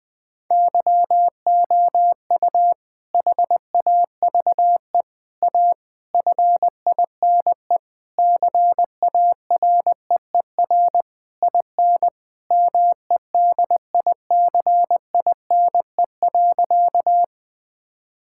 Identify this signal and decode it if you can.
Morse code, heard as a keyed tone, about 21 wpm, 700 Hz. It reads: YOU HAVE A FINE CAREER IN MEDICINE.
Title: Sentences from Top 1000 Words - Review Part 6 - 20wpm